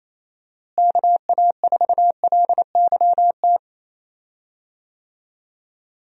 Morse code, heard as a keyed tone, about 27 wpm, 700 Hz. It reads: KA4LYT
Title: USA Call Signs - 6 Characters - Repeated 3x - 28wpm